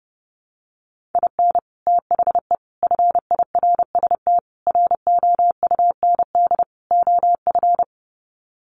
Morse code, heard as a keyed tone, about 30 wpm, 700 Hz. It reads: IN THE FIRST ROUND OF
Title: Sets of 5 Words Encore 30wpm